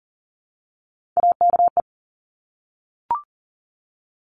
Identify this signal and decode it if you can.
Morse code, heard as a keyed tone, about 38 wpm, 700 Hz. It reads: AKE
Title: Top Three Letter Combinations in Words 40wpm